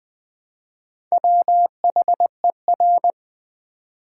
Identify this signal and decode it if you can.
Morse code, heard as a keyed tone, about 20 wpm, 700 Hz. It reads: WHER
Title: Build A Word - 2 Letters At A Time - 8-Letter Words - 20wpm